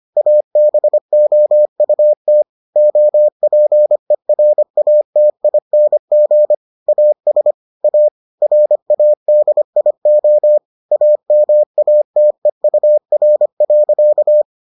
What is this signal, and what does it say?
Morse code, heard as a keyed tone, about 25 wpm, 600 Hz. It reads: ABOUT OPERATING AS A RADIO AMATEUR.